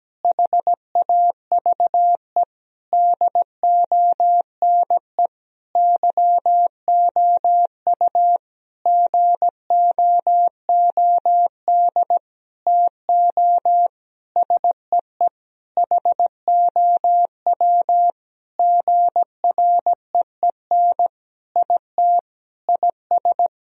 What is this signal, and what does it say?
Morse code, heard as a keyed tone, about 17 wpm, 700 Hz. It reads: HAVE DONE YOU GOOD TO SEE HOW GREEN IT IS
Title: Sentences from Top 1000 Words - Review Part 2 - 17wpm